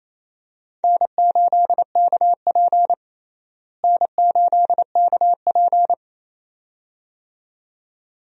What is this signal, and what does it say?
Morse code, heard as a keyed tone, about 28 wpm, 700 Hz. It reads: N8KP N8KP
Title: USA Call Signs - 4 Characters - Repeated 2x - 28wpm